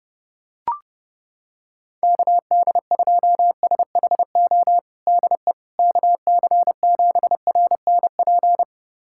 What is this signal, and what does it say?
Morse code, heard as a keyed tone, about 30 wpm, 700 Hz. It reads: KD2SHO DE KC7RNP
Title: USA Callsign DE Callsign - 6 Characters - 30wpm